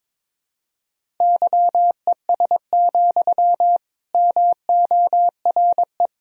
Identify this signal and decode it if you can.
Morse code, heard as a keyed tone, about 21 wpm, 700 Hz. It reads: YES, MORE
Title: Sentences from Top 1000 Words - Review Part 1 - 22wpm